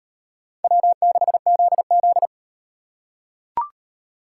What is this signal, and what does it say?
Morse code, heard as a keyed tone, about 38 wpm, 700 Hz. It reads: W6ZZ